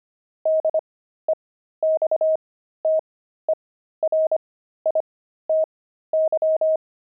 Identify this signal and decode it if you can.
Morse code, heard as a keyed tone, about 25 wpm, spaced as for 13 wpm, 630 Hz. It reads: DEXTERITY
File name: pangram-fs-13-cs-25-dexterity